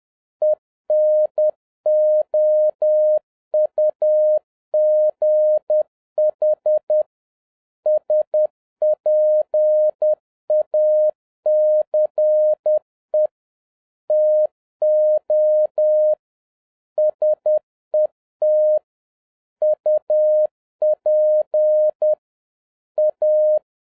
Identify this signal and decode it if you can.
Morse code, heard as a keyed tone, about 10 wpm, 610 Hz. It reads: ENOUGH SPACE TO SET UP A
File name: 2011-07-mp3 / n41 EXCITING QRP CW OPERATION 10wpm